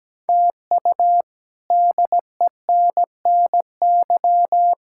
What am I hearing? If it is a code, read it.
Morse code, heard as a keyed tone, about 17 wpm, 700 Hz. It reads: TU DENNY
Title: K1USN SST QSO Practice - 17wpm